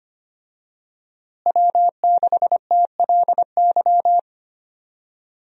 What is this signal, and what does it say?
Morse code, heard as a keyed tone, about 25 wpm, 700 Hz. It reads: W6TLY
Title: USA Call Signs - 5 Characters - Repeated 3x - 25wpm